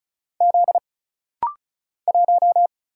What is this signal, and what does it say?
Morse code, heard as a keyed tone, about 36 wpm, 700 Hz. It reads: Z 1
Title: Single Letter-Number Alt - Rapid-Fire 35wpm